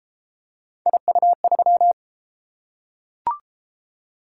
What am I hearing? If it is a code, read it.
Morse code, heard as a keyed tone, about 27 wpm, 700 Hz. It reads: IU3
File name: Speed-Racing - Three Letter-Number Combinations - 22wpm